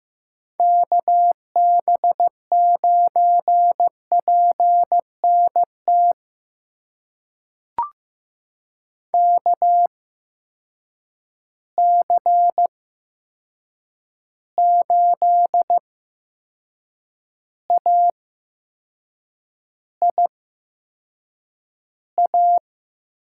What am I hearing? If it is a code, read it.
Morse code, heard as a keyed tone, about 15 wpm, spaced as for 6 wpm, 700 Hz. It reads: KB9PNT KC8AIA
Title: USA Call Signs - 6 Characters - 8x Character Spacing - 15wpm